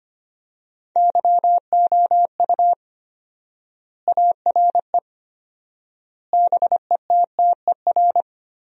Morse code, heard as a keyed tone, about 25 wpm, 700 Hz. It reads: YOU ARE BETTER
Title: 4x Word-Spacing - Sentences from Top 400 Words - 25wpm